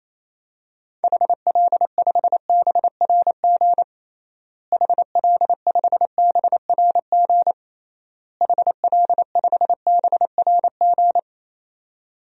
Morse code, heard as a keyed tone, about 28 wpm, 700 Hz. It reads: HL5BRG HL5BRG HL5BRG
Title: DXCC-100 Call Signs - 6 Characters - Repeated 3x - 28wpm